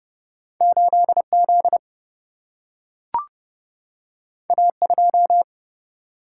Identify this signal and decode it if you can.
Morse code, heard as a keyed tone, about 30 wpm, 700 Hz. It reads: 8 Z A 2
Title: Single-Letter Single-Number 30wpm